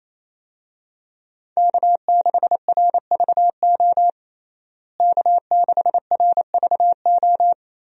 Morse code, heard as a keyed tone, about 28 wpm, 700 Hz. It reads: K6RVO K6RVO
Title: USA Call Signs - 5 Characters - Repeated 3x - 28wpm